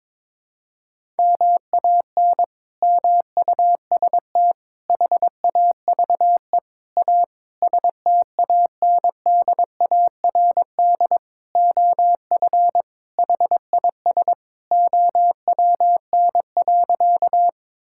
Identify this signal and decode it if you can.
Morse code, heard as a keyed tone, about 21 wpm, 700 Hz. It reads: MAN MUST HAVE A STANDARD OF HIS OWN.
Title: Sentences from Top 1000 Words - Review Part 6 - 22wpm